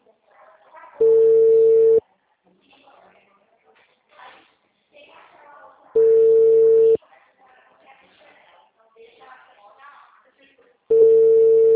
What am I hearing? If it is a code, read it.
Ringback tone (China, 450 Hz), with speech.